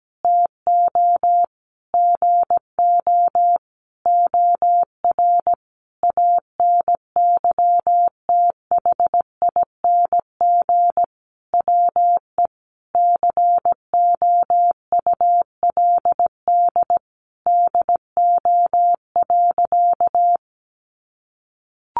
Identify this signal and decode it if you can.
Morse code, heard as a keyed tone, about 17 wpm, 700 Hz. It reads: TO GO OR ANYTHING WE COULD DO.
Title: Sentences from Top 1400 Words 17wpm